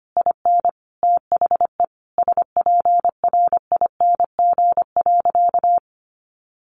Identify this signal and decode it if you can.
Morse code, heard as a keyed tone, about 25 wpm, 700 Hz. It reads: IN THE SPRING.
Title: Sentences from Top 1200 Words 25wpm